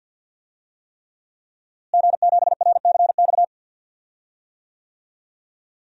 Morse code, heard as a keyed tone, about 51 wpm, 700 Hz. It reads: G7RCX
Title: DXCC-100 Call Signs - 5 Characters - Repeated 3x - 50wpm